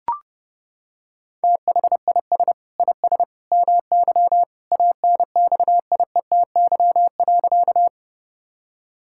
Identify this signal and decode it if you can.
Morse code, heard as a keyed tone, about 30 wpm, 700 Hz. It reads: THIS IS MY ANXIETY.
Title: Sentences from Top 1600 Words 30wpm